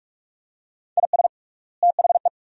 Morse code, heard as a keyed tone, about 47 wpm, 700 Hz. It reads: IS THE
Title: 3x Word-Spacing - Sentences from Top 400 Words - 45wpm